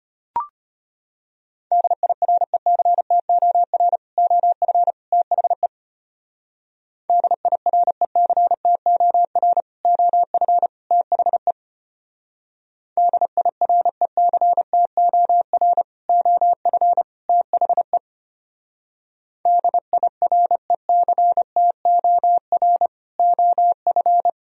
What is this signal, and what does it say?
Morse code, heard as a keyed tone, about 31 wpm, 700 Hz. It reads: DIRECTOR OF THE DIRECTOR OF THE DIRECTOR OF THE DIRECTOR OF